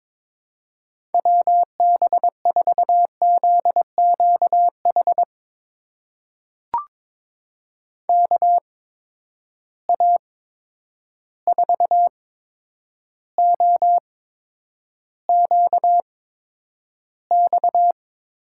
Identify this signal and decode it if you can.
Morse code, heard as a keyed tone, about 22 wpm, spaced as for 9 wpm, 700 Hz. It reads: WB4ZQH KA4OQX
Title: USA Call Signs - 6 Characters - 8x Character Spacing - 22wpm